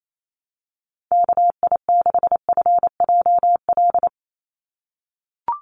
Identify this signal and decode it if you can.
Morse code, heard as a keyed tone, about 28 wpm, 700 Hz. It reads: KI6FJL
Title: USA Call Signs 28wpm